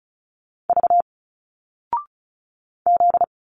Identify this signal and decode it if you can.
Morse code, heard as a keyed tone, about 34 wpm, 700 Hz. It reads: V Z
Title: Single Letters 35wpm